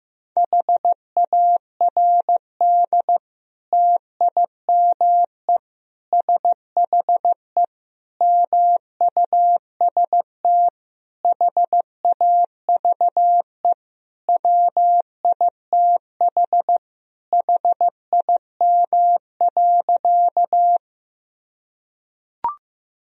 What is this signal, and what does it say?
Morse code, heard as a keyed tone, about 15 wpm, 700 Hz. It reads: HARD TIME SHE MUST HAVE WITH HIM.
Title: Sentences from Top 1000 Words - Review Part 1 - 15wpm